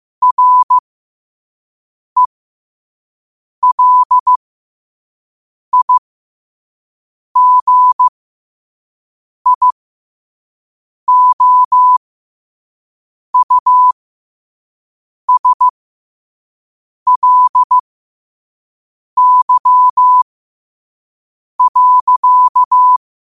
Morse code, heard as a keyed tone, about 15 wpm, spaced as for 5 wpm, 1000 Hz. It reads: RELIGIOUSLY.
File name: qotdfiles / 5wpm